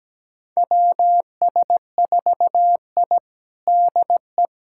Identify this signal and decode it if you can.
Morse code, heard as a keyed tone, about 17 wpm, 700 Hz. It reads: WS4I DE
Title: USA Callsign DE Callsign - 4 Characters - 17wpm